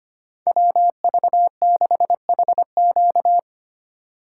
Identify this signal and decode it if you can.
Morse code, heard as a keyed tone, about 25 wpm, 700 Hz. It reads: WV6HQ
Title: USA Call Signs - 5 Characters - 2x Character Spacing - 25wpm